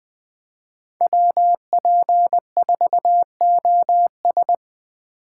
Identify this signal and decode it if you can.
Morse code, heard as a keyed tone, about 20 wpm, 700 Hz. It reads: WP4OS
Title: USA Call Signs - 5 Characters - 2x Character Spacing - 20wpm